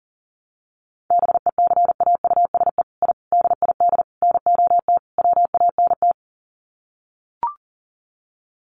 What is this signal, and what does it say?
Morse code, heard as a keyed tone, about 40 wpm, 700 Hz. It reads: BECAUSE I DID NOT WANT
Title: Sets of 5 Words Encore2 40wpm